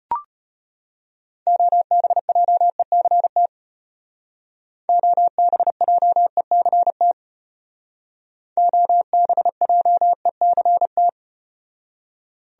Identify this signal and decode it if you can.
Morse code, heard as a keyed tone, about 36 wpm, 700 Hz. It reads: OBJECT OBJECT OBJECT